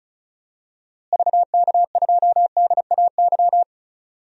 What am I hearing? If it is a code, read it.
Morse code, heard as a keyed tone, about 36 wpm, 700 Hz. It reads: VK2DAY
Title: DXCC-100 Call Signs - 6 Characters - Repeated 3x - 35wpm